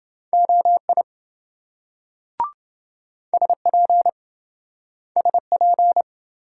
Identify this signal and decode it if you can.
Morse code, heard as a keyed tone, about 27 wpm, 700 Hz. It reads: OI SP SP